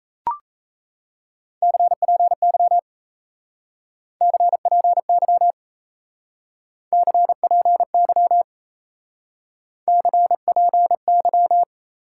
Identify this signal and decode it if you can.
Morse code, heard as a keyed tone, about 37 wpm, 700 Hz. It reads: CPY CPY CPY CPY